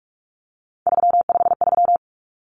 Morse code, heard as a keyed tone, about 44 wpm, 700 Hz. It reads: 353
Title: Numbers - 3-Digit 45wpm